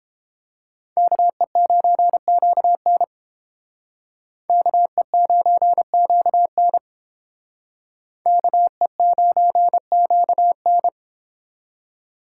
Morse code, heard as a keyed tone, about 30 wpm, 700 Hz. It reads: KE9QN KE9QN KE9QN